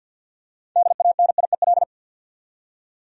Morse code, heard as a keyed tone, about 46 wpm, 700 Hz. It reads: DANIEL
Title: USA Common Names - 50wpm